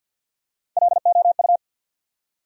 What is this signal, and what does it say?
Morse code, heard as a keyed tone, about 53 wpm, 700 Hz. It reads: LOU
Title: USA Common Names - Max 3 Letters - 50wpm